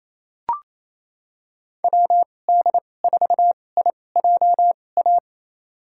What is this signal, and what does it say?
Morse code, heard as a keyed tone, about 28 wpm, spaced as for 20 wpm, 700 Hz. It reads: WD4IJA
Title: USA Call Signs - 6 Characters - 2x Character Spacing - 28wpm